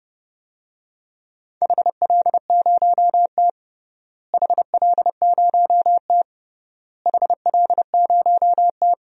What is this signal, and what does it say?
Morse code, heard as a keyed tone, about 30 wpm, 700 Hz. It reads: HL0T HL0T HL0T
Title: DXCC-100 Call Signs - 4 Characters - Repeated 3x - 30wpm